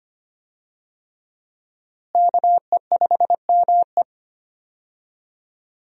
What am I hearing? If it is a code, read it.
Morse code, heard as a keyed tone, about 25 wpm, 700 Hz. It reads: KE5ME
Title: USA Call Signs - 5 Characters - Repeated 3x - 25wpm